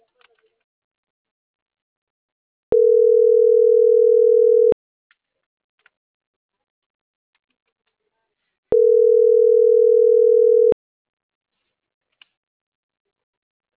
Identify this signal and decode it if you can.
Ringback tone (North America), 440 + 480 Hz.